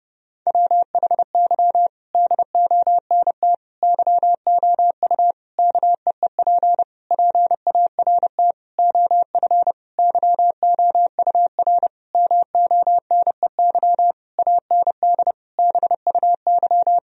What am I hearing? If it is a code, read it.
Morse code, heard as a keyed tone, about 30 wpm, 700 Hz. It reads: WHY DONT YOU KEEP PART OF YOUR MONEY AND BUY
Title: Sentences from Top 1000 Words - Review Part 4 - 30wpm